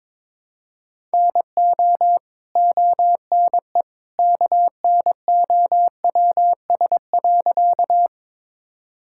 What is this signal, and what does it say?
Morse code, heard as a keyed tone, about 22 wpm, 700 Hz. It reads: NO ONE KNOWS.
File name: practice / 22wpm / Sentences with 2-Word Phrases - 22wpm